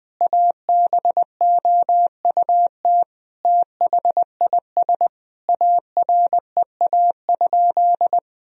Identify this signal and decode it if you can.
Morse code, heard as a keyed tone, about 20 wpm, 700 Hz. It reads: ABOUT THIS AREA?
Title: Sentences from Top 1000 Words - Review Part 2 - 20wpm